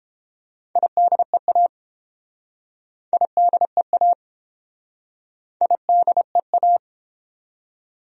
Morse code, heard as a keyed tone, about 30 wpm, 700 Hz. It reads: IDEA IDEA IDEA